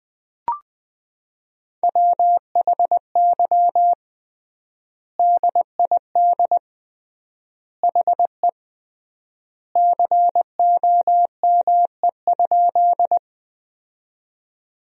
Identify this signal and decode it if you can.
Morse code, heard as a keyed tone, about 20 wpm, 700 Hz. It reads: WHY DID HE COME?
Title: 3x Word-Spacing - Sentences from Top 200 Words - 20wpm